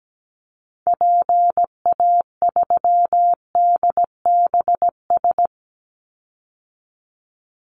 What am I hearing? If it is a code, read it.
Morse code, heard as a keyed tone, about 17 wpm, 700 Hz. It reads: PA3DBS
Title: Call Signs - 17wpm